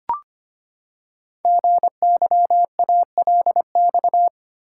Morse code, heard as a keyed tone, about 25 wpm, 700 Hz. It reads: GYALX
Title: Code Groups - 5 Letters - 25wpm